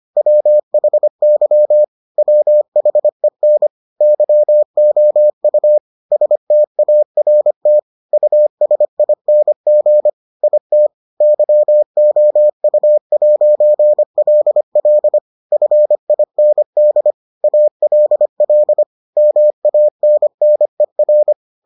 Morse code, heard as a keyed tone, about 25 wpm, 600 Hz. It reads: WHY WHEN YOU START USING IT YOU'LL FIND ALL MANNER